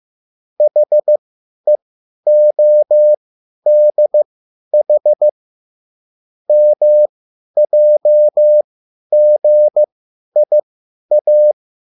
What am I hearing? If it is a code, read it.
Morse code, heard as a keyed tone, about 15 wpm, spaced as for 10 wpm, 600 Hz. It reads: HEODH MJGIA
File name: QSL / CW lesson